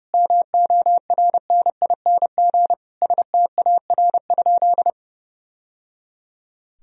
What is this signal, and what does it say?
Morse code, heard as a keyed tone, about 30 wpm, 700 Hz. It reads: MORNING STAR?
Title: Sentences from Top 1000 Words - Review Part 2 - 30wpm